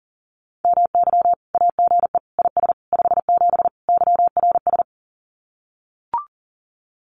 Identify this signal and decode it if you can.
Morse code, heard as a keyed tone, about 40 wpm, 700 Hz. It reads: MY AGE IS 57 YRS
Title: Common QSO Phrases Encore 40wpm